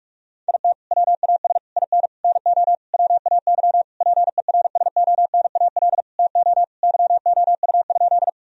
Morse code, heard as a keyed tone, about 46 wpm, 700 Hz. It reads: IT WAS IN NO WAY PERSONAL TO YOU?
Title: Sentences from Top 1000 Words - Review Part 4 - 45wpm